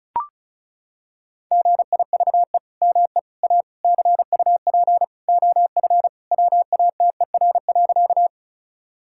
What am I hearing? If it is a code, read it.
Morse code, heard as a keyed tone, about 36 wpm, 700 Hz. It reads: GIVE ME A CUP OF WATER.
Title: Sentences from Top 1600 Words 35wpm